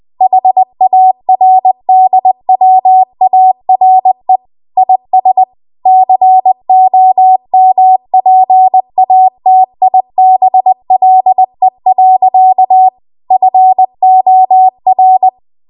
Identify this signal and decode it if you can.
Morse code, heard as a keyed tone, about 20 wpm, 750 Hz. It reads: HARDWARE IS COMPATIBLE. FOR
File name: Archive / 20 WPM / 20WPM